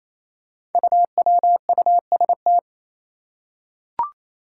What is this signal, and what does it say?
Morse code, heard as a keyed tone, about 28 wpm, 700 Hz. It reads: UWUST
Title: Five Challenging Letter-Number Combinations 28wpm